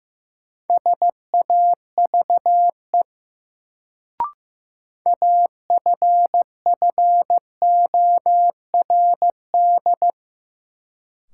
Morse code, heard as a keyed tone, about 16 wpm, 700 Hz. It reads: SAVE AFFORD
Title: Sentences from Top 1100 Words - New Word Drill - 15wpm